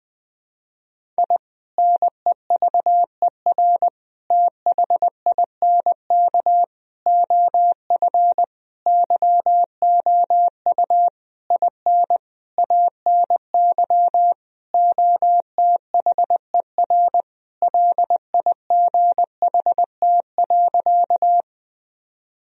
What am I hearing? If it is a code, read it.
Morse code, heard as a keyed tone, about 21 wpm, 700 Hz. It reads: I NEVER THINK OF YOU IN ANY OTHER LIGHT.
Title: Sentences from Top 1000 Words - Review Part 1 - 20wpm